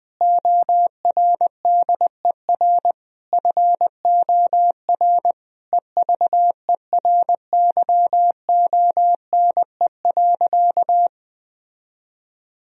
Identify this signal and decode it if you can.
Morse code, heard as a keyed tone, about 21 wpm, 700 Hz. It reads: ORDER FOR EVERYONE.
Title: Sentences from Top 1000 Words - Review Part 3 - 20wpm